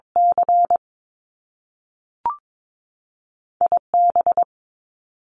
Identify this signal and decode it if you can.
Morse code, heard as a keyed tone, about 22 wpm, 700 Hz. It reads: C I B
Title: Top Two Letter Combinations in Words 22wpm